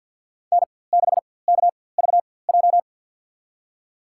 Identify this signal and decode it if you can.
Morse code, heard as a keyed tone, about 52 wpm, spaced as for 23 wpm, 700 Hz. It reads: N6KVJ
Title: USA Call Signs - 5 Characters - 4x Character Spacing - 50wpm